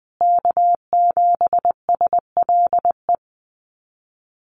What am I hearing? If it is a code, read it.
Morse code, heard as a keyed tone, about 20 wpm, 700 Hz. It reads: K7SLE
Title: USA Call Signs Repeated 3x 20wpm